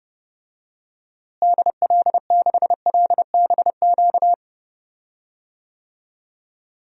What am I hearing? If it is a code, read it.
Morse code, heard as a keyed tone, about 30 wpm, 700 Hz. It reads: DL6LBQ
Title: DXCC-100 Call Signs - 6 Characters - Repeated 3x - 30wpm